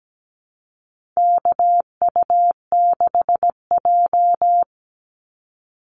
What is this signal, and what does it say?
Morse code, heard as a keyed tone, about 17 wpm, 700 Hz. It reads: KU6J
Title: NA SOTA Activator Callsigns 17wpm